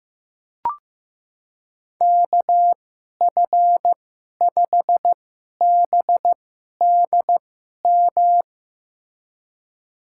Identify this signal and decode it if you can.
Morse code, heard as a keyed tone, about 16 wpm, spaced as for 11 wpm, 700 Hz. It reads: KF5BDM
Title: USA Call Signs - 6 Characters - 2x Character Spacing - 15wpm